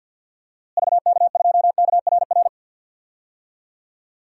Morse code, heard as a keyed tone, about 52 wpm, 700 Hz. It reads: VK2KRR
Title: DXCC-100 Call Signs - 6 Characters - Repeated 2x - 50wpm